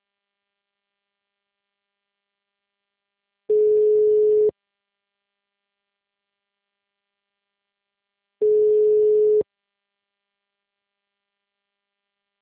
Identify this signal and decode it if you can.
Ringback tone (Europe, 425 Hz).